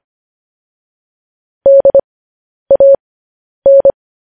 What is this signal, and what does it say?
Morse code, heard as a keyed tone, about 25 wpm, spaced as for 10 wpm, 550 Hz. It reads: DAN
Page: news headlines in morse code at 10 wpm